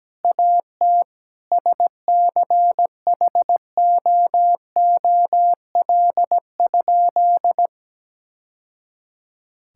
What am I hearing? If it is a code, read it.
Morse code, heard as a keyed tone, about 17 wpm, 700 Hz. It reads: AT SCHOOL?
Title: Sentences from Top 1000 Words - Review Part 6 - 17wpm